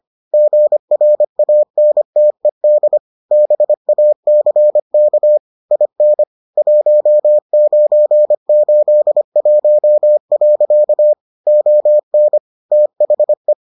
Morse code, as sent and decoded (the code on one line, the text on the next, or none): --. .-. .- -. - . -.. / -... .- -.-. -.- / .. -. / .---- ----. ---.. .---- .-.-.- / --- -. / - .... .
GRANTED BACK IN 1981. ON THE